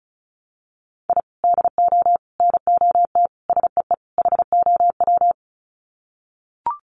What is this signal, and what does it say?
Morse code, heard as a keyed tone, about 34 wpm, 700 Hz. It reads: I DO NOT SEE HOW
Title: Sets of 5 Words 35wpm